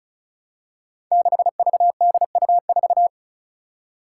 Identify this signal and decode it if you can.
Morse code, heard as a keyed tone, about 36 wpm, 700 Hz. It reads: 6VDU4
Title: Five Challenging Letter-Number Combinations 35wpm